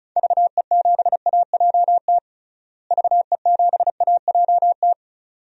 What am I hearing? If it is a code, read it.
Morse code, heard as a keyed tone, about 36 wpm, 700 Hz. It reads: VE7AJT VE7AJT
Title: DXCC-100 Call Signs - 6 Characters - Repeated 2x - 35wpm